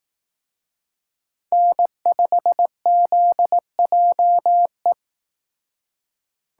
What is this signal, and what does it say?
Morse code, heard as a keyed tone, about 18 wpm, 700 Hz. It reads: N5ZJE
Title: Speed Racing - USA 1x3 Call Signs 15wpm